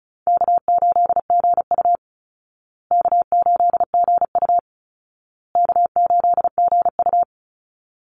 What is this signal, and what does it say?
Morse code, heard as a keyed tone, about 34 wpm, 700 Hz. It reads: K8GU K8GU K8GU
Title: Easy Call Signs Repeated 3x - 35wpm